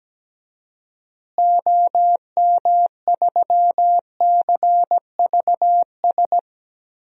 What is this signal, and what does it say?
Morse code, heard as a keyed tone, about 17 wpm, 700 Hz. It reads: OM3CVS
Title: DXCC-100 Call Signs - 6 Characters - Repeated 3x - 17wpm